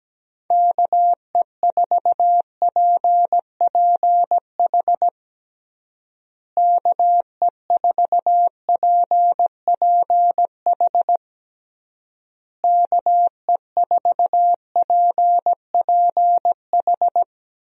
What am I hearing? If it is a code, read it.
Morse code, heard as a keyed tone, about 17 wpm, 700 Hz. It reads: KE4PPH KE4PPH KE4PPH
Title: USA Call Signs - 6 Characters - Repeated 3x - 17wpm